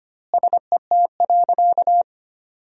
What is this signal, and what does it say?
Morse code, heard as a keyed tone, about 25 wpm, 700 Hz. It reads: SET.
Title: 2x Word-Spacing - Sentences from Top 400 Words - 25wpm